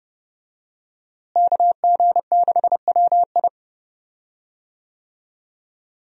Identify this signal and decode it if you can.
Morse code, heard as a keyed tone, about 28 wpm, 700 Hz. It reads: KG6WI
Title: USA Call Signs - 5 Characters - Repeated 3x - 30wpm